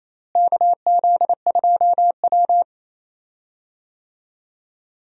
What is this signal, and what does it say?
Morse code, heard as a keyed tone, about 28 wpm, 700 Hz. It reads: KZ2W
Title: USA Call Signs - 4 Characters - Repeated 3x - 28wpm